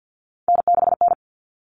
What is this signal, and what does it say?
Morse code, heard as a keyed tone, about 50 wpm, 700 Hz. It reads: N6N
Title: All Cut Numbers - 3-Digit 50wpm